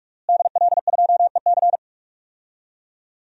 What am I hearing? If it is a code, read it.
Morse code, heard as a keyed tone, about 43 wpm, 700 Hz. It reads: DL2EC
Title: DXCC-100 Call Signs - 5 Characters - Repeated 3x - 45wpm